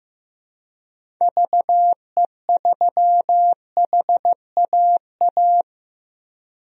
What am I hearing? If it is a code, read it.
Morse code, heard as a keyed tone, about 15 wpm, 700 Hz. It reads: VE3HAA
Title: DXCC-100 Call Signs - 6 Characters - Repeated 2x - 15wpm